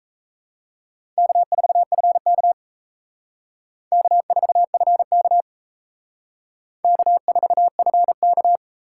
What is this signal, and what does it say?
Morse code, heard as a keyed tone, about 32 wpm, 700 Hz. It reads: K4FK K4FK K4FK
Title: Speed Racing - USA 1x2 Call Signs 28wpm